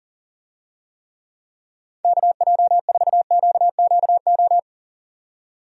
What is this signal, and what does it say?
Morse code, heard as a keyed tone, about 40 wpm, 700 Hz. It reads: KJ4QQO
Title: USA Call Signs - 6 Characters - 40wpm